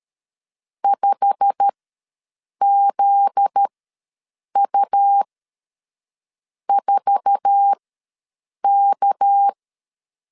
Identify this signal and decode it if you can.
Morse code, heard as a keyed tone, about 13 wpm, spaced as for 6 wpm, 800 Hz. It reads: 5ZU4K
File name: LEZIONI / 9 2 ws 12wpm 3406